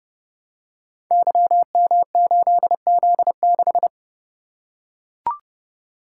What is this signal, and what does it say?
Morse code, heard as a keyed tone, about 30 wpm, 700 Hz. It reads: YM8Z6
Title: Code Groups - 5 Letter:Numbers - 30wpm